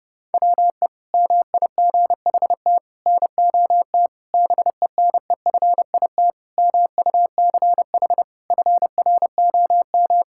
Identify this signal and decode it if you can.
Morse code, heard as a keyed tone, about 32 wpm, 700 Hz. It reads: WE MIGHT NOT BENEFIT MUCH FROM
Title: Sentences from Top 1000 Words - Review Part 6 - 30wpm